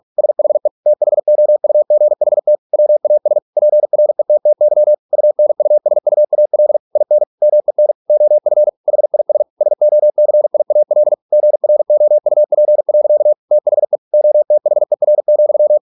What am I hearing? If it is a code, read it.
Morse code, heard as a keyed tone, about 46 wpm, 600 Hz. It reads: SHE THOUGHT WAS PRETTY UNUSUAL IN MEN OF HIS SOCIAL GROUP. THE OTHER,